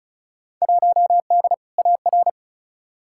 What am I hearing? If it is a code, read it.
Morse code, heard as a keyed tone, about 36 wpm, 700 Hz. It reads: 1D AR
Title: Field Day Exchanges Encore - 35wpm